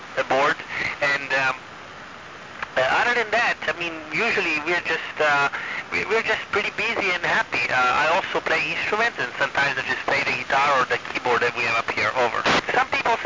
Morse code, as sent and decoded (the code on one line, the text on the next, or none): none